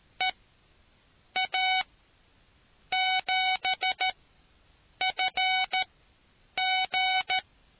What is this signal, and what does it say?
Morse code, heard as a keyed tone, about 14 wpm, spaced as for 7 wpm, 750 Hz. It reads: EA7FG